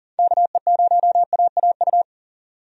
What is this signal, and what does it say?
Morse code, heard as a keyed tone, about 40 wpm, 700 Hz. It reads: KE0AAU
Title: USA Call Signs - 6 Characters - 40wpm